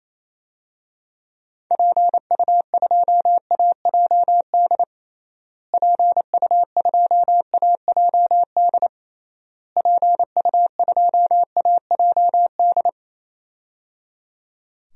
Morse code, heard as a keyed tone, about 28 wpm, 700 Hz. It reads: PU2AJD PU2AJD PU2AJD
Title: DXCC-100 Call Signs - 6 Characters - Repeated 3x - 28wpm